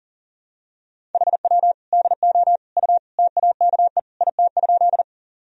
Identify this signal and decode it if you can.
Morse code, heard as a keyed tone, about 38 wpm, 700 Hz. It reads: HW DO U TAKE IT?
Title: Sentences from Top 200 Words and Abbreviations 40wpm